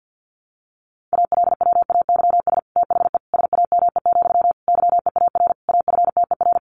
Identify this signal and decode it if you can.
Morse code, heard as a keyed tone, about 50 wpm, 700 Hz. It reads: ALWAYS THE SAME, YEAR AFTER
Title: Sentences from Top 300 Words 50wpm